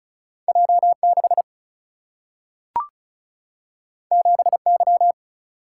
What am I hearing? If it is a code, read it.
Morse code, heard as a keyed tone, about 36 wpm, 700 Hz. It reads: J6 7Y
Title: Single-Letter Single-Number 35wpm